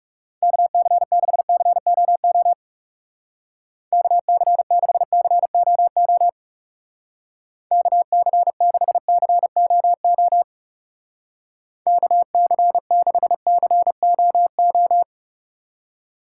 Morse code, heard as a keyed tone, about 36 wpm, 700 Hz. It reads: KC6COO KC6COO KC6COO KC6COO
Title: Speed Racing - USA 2x3 Call Signs 30wpm